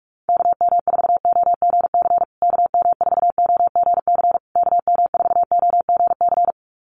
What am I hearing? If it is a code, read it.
Morse code, heard as a keyed tone, about 44 wpm, 700 Hz. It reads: KM4OGC KM4OGC KM4OGC
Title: USA Call Signs Repeated 3x 45wpm